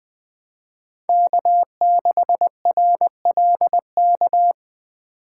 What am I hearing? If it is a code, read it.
Morse code, heard as a keyed tone, about 20 wpm, 700 Hz. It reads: K6RLK
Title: USA Call Signs - 5 Characters - 3x Character Spacing - 20wpm